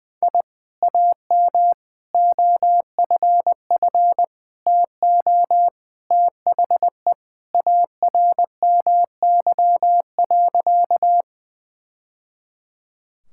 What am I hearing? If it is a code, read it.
Morse code, heard as a keyed tone, about 20 wpm, 700 Hz. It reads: I AM OFF TO THE ARMY.
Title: Sentences from Top 1000 Words - Review Part 5 - 20wpm